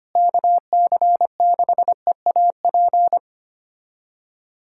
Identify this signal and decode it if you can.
Morse code, heard as a keyed tone, about 25 wpm, 700 Hz. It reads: KC6EAP
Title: USA Callsign DE Callsign 25wpm